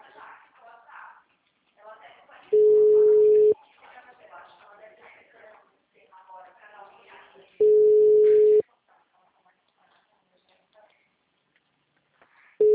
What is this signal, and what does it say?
Ringback tone (Europe, 425 Hz), with speech.